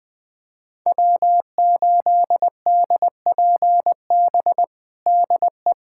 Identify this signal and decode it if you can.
Morse code, heard as a keyed tone, about 20 wpm, 700 Hz. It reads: W8DPB DE
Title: USA Callsign DE Callsign - 5 Characters - 20wpm